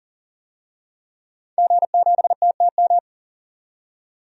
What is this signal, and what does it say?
Morse code, heard as a keyed tone, about 40 wpm, 700 Hz. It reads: G7TTM
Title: DXCC-100 Call Signs - 5 Characters - 40wpm